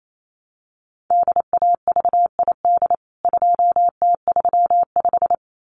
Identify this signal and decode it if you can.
Morse code, heard as a keyed tone, about 28 wpm, 700 Hz. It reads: DAVID 2T35
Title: CWT Exchange 28wpm